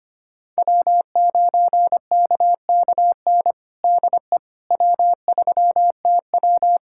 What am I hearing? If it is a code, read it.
Morse code, heard as a keyed tone, about 25 wpm, 700 Hz. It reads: W9KKN DE W3TW
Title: Practice QSO - Exchange 4 - Ending - 25wpm